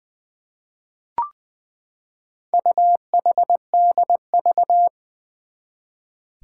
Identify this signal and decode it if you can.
Morse code, heard as a keyed tone, about 20 wpm, 700 Hz. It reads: UHDV
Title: Four Challenging Letter-Number Combinations 20wpm